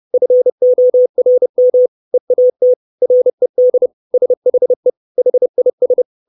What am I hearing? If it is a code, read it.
Morse code, heard as a keyed tone, about 30 wpm, 500 Hz. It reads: FORM EAT RED SHE HIS
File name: 632 Words 30WPM